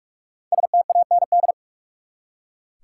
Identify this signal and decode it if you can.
Morse code, heard as a keyed tone, about 46 wpm, 700 Hz. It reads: STAND
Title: Five Letter Words - 45wpm